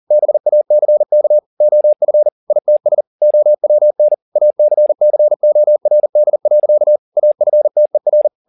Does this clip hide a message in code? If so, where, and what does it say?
Yes: Morse code from 0.1 to 8.3 s, heard as a keyed tone, about 40 wpm, 600 Hz. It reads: BACK OF ITS OWN ACCORD. AFTER